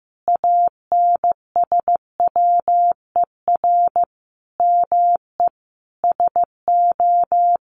Morse code, heard as a keyed tone, about 15 wpm, 700 Hz. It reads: ANSWER ME SO
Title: Sentences from Top 1500 Words 15wpm